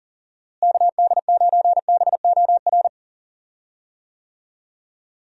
Morse code, heard as a keyed tone, about 40 wpm, 700 Hz. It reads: KD9BOR
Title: USA Call Signs - 6 Characters - Repeated 3x - 40wpm